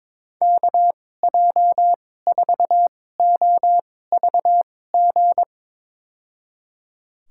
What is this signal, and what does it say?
Morse code, heard as a keyed tone, about 22 wpm, spaced as for 16 wpm, 700 Hz. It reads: KJ4OVG
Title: USA Call Signs - 6 Characters - 2x Character Spacing - 22wpm